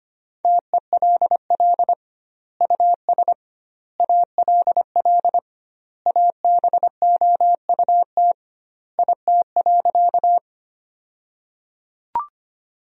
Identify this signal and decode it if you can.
Morse code, heard as a keyed tone, about 25 wpm, 700 Hz. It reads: TELL US ALL ABOUT IT.
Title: 2x Word-Spacing - Sentences from Top 200 Words - 25wpm